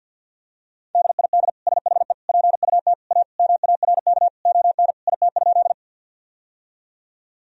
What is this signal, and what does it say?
Morse code, heard as a keyed tone, about 46 wpm, 700 Hz. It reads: DID SHE PUT A MARK ON IT?
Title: Sentences from Top 1000 Words - Review Part 1 - 50wpm